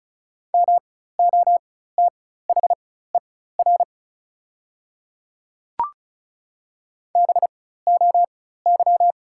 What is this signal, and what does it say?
Morse code, heard as a keyed tone, about 36 wpm, spaced as for 16 wpm, 690 Hz. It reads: MOTHER BOY
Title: 4x Character Spacing - Top 500 Words - 35wpm